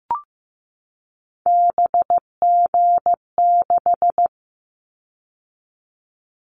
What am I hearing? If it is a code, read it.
Morse code, heard as a keyed tone, about 15 wpm, 700 Hz. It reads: BG6
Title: Callsign Prefixes for Top 100 DXCC - 15wpm